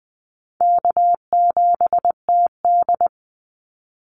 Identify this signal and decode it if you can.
Morse code, heard as a keyed tone, about 20 wpm, 700 Hz. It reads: K7TD
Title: Easy Call Signs Repeated 3x - 20wpm